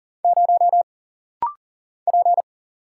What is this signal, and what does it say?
Morse code, heard as a keyed tone, about 43 wpm, 700 Hz. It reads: 0 P
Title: Single Letter-Number - Rapid-Fire 40wpm